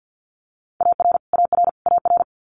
Morse code, heard as a keyed tone, about 49 wpm, 700 Hz. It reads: AR AR AR
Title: US States Repeated 3x 50wpm